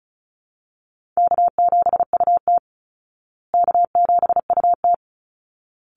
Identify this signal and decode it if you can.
Morse code, heard as a keyed tone, about 34 wpm, 700 Hz. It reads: K7UT K7UT
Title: Easy Call Signs Repeated 2x - 35wpm